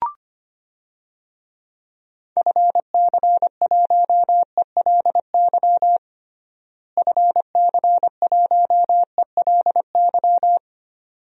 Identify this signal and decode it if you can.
Morse code, heard as a keyed tone, about 25 wpm, 700 Hz. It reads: FC1ELY FC1ELY
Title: DXCC-100 Call Signs - 6 Characters - Repeated 2x - 25wpm